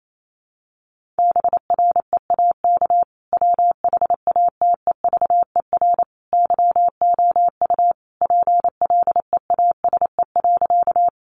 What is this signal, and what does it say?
Morse code, heard as a keyed tone, about 28 wpm, 700 Hz. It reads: BREAK WHATEVER YOU PLEASE.
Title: Sentences from Top 1500 Words 28wpm